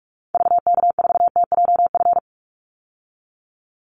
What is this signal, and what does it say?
Morse code, heard as a keyed tone, about 44 wpm, 700 Hz. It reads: VK4TJF
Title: Call Signs - 45wpm